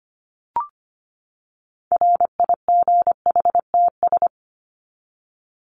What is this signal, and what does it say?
Morse code, heard as a keyed tone, about 25 wpm, 700 Hz. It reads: RIGHTS
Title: ES:S Suffix - Top 100 Words with Base Words 25wpm